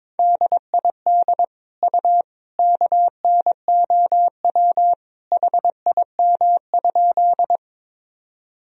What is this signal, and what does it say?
Morse code, heard as a keyed tone, about 22 wpm, 700 Hz. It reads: DID U KNOW HIM?
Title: Sentences from Top 100 Words and Abbreviations 22wpm